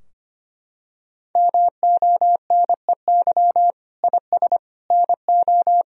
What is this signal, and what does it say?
Morse code, heard as a keyed tone, about 25 wpm, 700 Hz. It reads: MONEY IS NO
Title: Sentences from Top 1000 Words - Review Part 3 - 25wpm